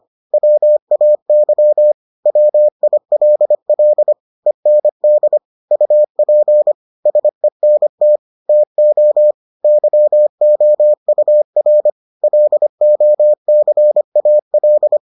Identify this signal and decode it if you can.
Morse code, heard as a keyed tone, about 25 wpm, 600 Hz. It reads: WAY WILL END UP SENT TO YOUR LOCAL